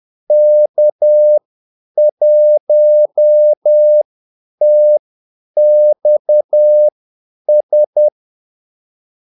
Morse code, heard as a keyed tone, about 10 wpm, spaced as for 8 wpm, 600 Hz. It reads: K1TXS
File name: Callsigns-10-8wpm